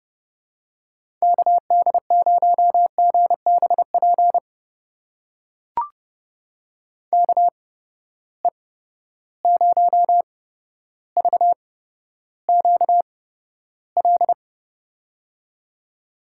Morse code, heard as a keyed tone, about 31 wpm, spaced as for 13 wpm, 700 Hz. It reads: KD0GBP KE0VQL
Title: USA Call Signs - 6 Characters - 8x Character Spacing - 30wpm